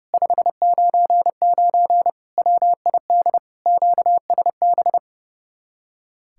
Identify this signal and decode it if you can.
Morse code, heard as a keyed tone, about 37 wpm, 700 Hz. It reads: 599 WID QSB
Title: Speed-Racing Common QSO Phrases Encore - 30wpm